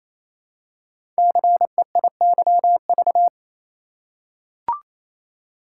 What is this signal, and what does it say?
Morse code, heard as a keyed tone, about 27 wpm, 700 Hz. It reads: CEIYV
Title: Code Groups - 5 Letter:Numbers - 28wpm